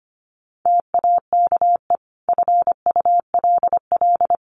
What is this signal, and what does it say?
Morse code, heard as a keyed tone, about 25 wpm, 700 Hz. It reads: TAKE FULL
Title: Sentences from Top 1400 Words 25wpm